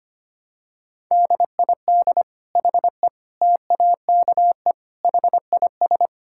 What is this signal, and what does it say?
Morse code, heard as a keyed tone, about 25 wpm, 700 Hz. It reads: DID HE TAKE HIS
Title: Sentences from Top 1000 Words - Review Part 6 - 25wpm